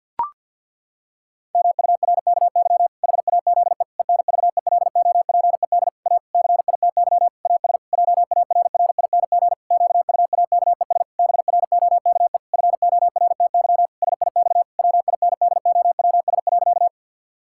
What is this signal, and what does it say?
Morse code, heard as a keyed tone, about 47 wpm, 700 Hz. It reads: MURKY HAZE ENVELOPED A CITY AS JARRING QUAKES BROKE FORTY SIX WINDOWS.
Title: Pangram Challenge v2 - 50wpm